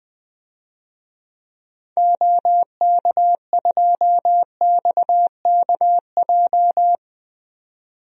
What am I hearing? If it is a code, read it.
Morse code, heard as a keyed tone, about 21 wpm, 700 Hz. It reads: OK2XKJ
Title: DXCC-100 Call Signs - 6 Characters - 20wpm